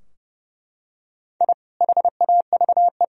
Morse code, heard as a keyed tone, about 30 wpm, 700 Hz. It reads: I HAVE
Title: Sentences from Top 1000 Words - Review Part 2 - 30wpm